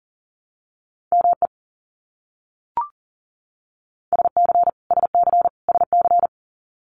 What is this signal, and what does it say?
Morse code, heard as a keyed tone, about 40 wpm, 700 Hz. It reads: ME SC SC SC
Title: US States Repeated 3x 40wpm